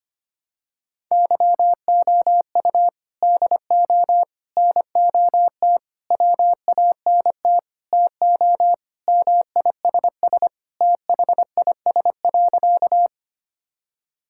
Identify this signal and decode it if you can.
Morse code, heard as a keyed tone, about 25 wpm, 700 Hz. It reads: YOU DO NOT WANT TO MISS THIS.
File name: practice / 25wpm / Sentences with 4-Word Phrases - 25wpm